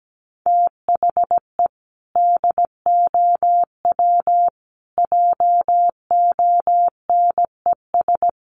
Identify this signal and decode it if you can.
Morse code, heard as a keyed tone, about 17 wpm, 700 Hz. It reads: THE DOW JONES
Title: Sets of 5 Words 17wpm